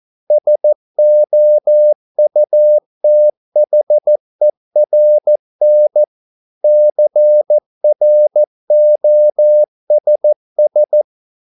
Morse code, heard as a keyed tone, about 14 wpm, 600 Hz. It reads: SOUTHERN CROSS